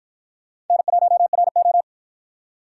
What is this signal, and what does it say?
Morse code, heard as a keyed tone, about 53 wpm, 700 Hz. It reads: N1RO